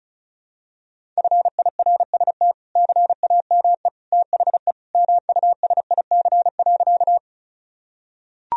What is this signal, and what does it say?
Morse code, heard as a keyed tone, about 36 wpm, 700 Hz. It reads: FIRST CAME THE MUSIC.
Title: Sentences from Top 1000 Words - Review Part 1 - 35wpm